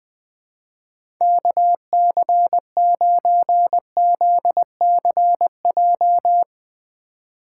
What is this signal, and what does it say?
Morse code, heard as a keyed tone, about 20 wpm, 700 Hz. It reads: KC9ZCJ
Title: USA Call Signs - 3x Character Spacing - 20wpm